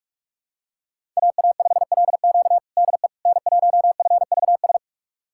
Morse code, heard as a keyed tone, about 46 wpm, 700 Hz. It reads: AA5LQ DE N1FVS
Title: USA Callsign DE Callsign - 5 Characters - 45wpm